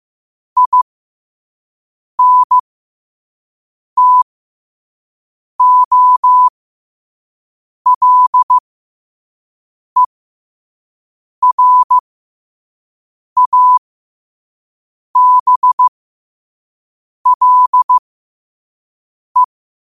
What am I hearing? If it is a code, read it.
Morse code, heard as a keyed tone, about 15 wpm, spaced as for 5 wpm, 1000 Hz. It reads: INTOLERABLE
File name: qotdfiles / 5wpm